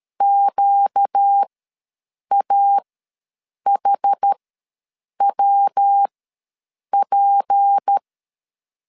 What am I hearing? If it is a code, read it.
Morse code, heard as a keyed tone, about 13 wpm, spaced as for 7 wpm, 800 Hz. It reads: QAHWP
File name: LEZIONI / 6 PKX0 / 6 P ws 12wpm 3117